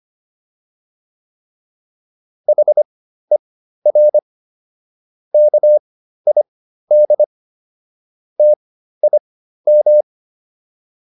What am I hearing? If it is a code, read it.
Morse code, heard as a keyed tone, about 25 wpm, spaced as for 13 wpm, 600 Hz. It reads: HER KID TIM